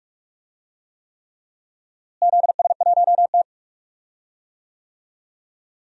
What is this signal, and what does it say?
Morse code, heard as a keyed tone, about 41 wpm, 700 Hz. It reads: ZS1T
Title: DXCC-100 Call Signs - 4 Characters - Repeated 3x - 45wpm